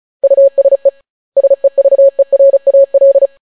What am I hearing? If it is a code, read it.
Morse code, heard as a keyed tone, about 35 wpm, 550 Hz. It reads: USE SEVERAL